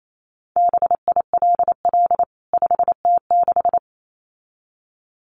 Morse code, heard as a keyed tone, about 28 wpm, 700 Hz. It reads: BILL 5T6
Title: CWT Exchange 28wpm